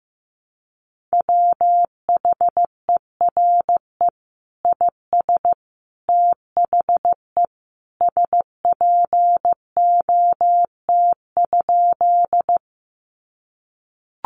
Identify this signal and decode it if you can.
Morse code, heard as a keyed tone, about 15 wpm, 700 Hz. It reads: WHERE IS THE SPOT?
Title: Sentences from Top 1100 Words 15wpm